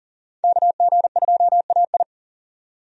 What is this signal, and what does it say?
Morse code, heard as a keyed tone, about 40 wpm, 700 Hz. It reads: KG2AI
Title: USA Call Signs - 5 Characters - 40wpm